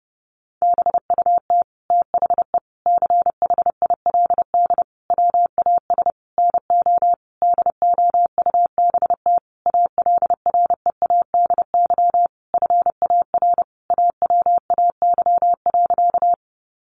Morse code, heard as a keyed tone, about 30 wpm, 700 Hz. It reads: BUT THE CHILD WAS NO DOUBT ALREADY FAR AWAY.
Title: Sentences from Top 1500 Words 30wpm